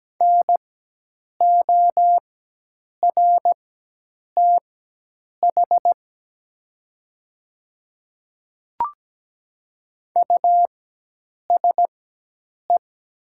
Morse code, heard as a keyed tone, about 17 wpm, spaced as for 8 wpm, 700 Hz. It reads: NORTH USE
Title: 4x Character Spacing - Top 500 Words - 17wpm